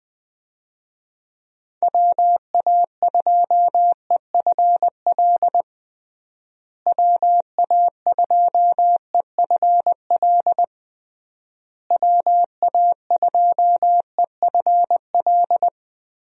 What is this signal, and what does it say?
Morse code, heard as a keyed tone, about 20 wpm, 700 Hz. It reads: WA2EFL WA2EFL WA2EFL
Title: USA Call Signs - 6 Characters - Repeated 3x - 20wpm